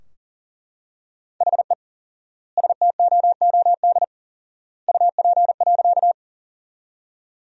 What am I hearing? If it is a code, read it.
Morse code, heard as a keyed tone, about 40 wpm, 700 Hz. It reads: HE STOOD UP.
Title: 4x Word-Spacing - Sentences from Top 500 Words - 40wpm